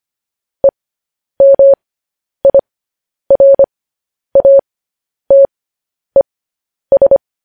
Morse code, heard as a keyed tone, about 25 wpm, spaced as for 10 wpm, 550 Hz. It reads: EMIRATES